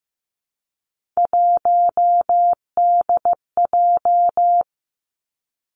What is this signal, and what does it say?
Morse code, heard as a keyed tone, about 15 wpm, 700 Hz. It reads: 1DJ
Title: Three Challenging Letter-Number Combinations 15wpm